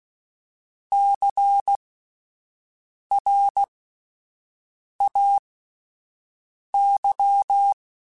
Morse code, heard as a keyed tone, about 16 wpm, spaced as for 5 wpm, 800 Hz. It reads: CRAY
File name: Spaced Alphas / Spaced Alphas mp3s / Cs sp @10